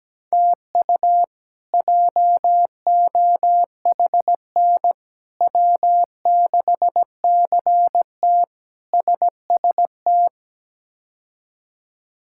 Morse code, heard as a keyed tone, about 17 wpm, 700 Hz. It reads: TU JOHN W6CT SST
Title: K1USN SST QSO Practice - 17wpm